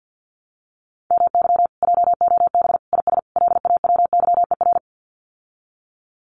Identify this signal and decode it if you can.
Morse code, heard as a keyed tone, about 48 wpm, 700 Hz. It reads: MY JOB IS LAWYER
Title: Common QSO Phrases Encore 50wpm